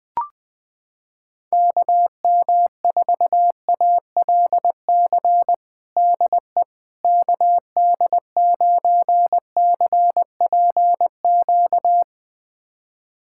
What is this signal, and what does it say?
Morse code, heard as a keyed tone, about 20 wpm, 700 Hz. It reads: KM4ALC DE KD9CPQ
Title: USA Callsign DE Callsign 20wpm